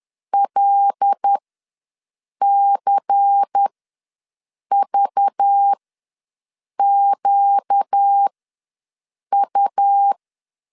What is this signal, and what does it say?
Morse code, heard as a keyed tone, about 10 wpm, spaced as for 6 wpm, 800 Hz. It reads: LCVQU
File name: LEZIONI / 5 LBVC / 5 LBVC ws 10wpm 3733